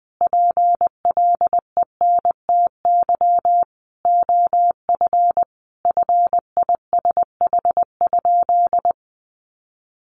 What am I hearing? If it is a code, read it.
Morse code, heard as a keyed tone, about 20 wpm, 700 Hz. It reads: PLENTY OF FISH?
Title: Sentences from Top 1500 Words 20wpm